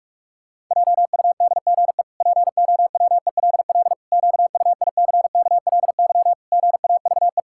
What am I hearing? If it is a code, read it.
Morse code, heard as a keyed tone, about 41 wpm, 700 Hz. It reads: JUDGE POWELL QUICKLY GAVE
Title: Pangram Challenge v2 - 45wpm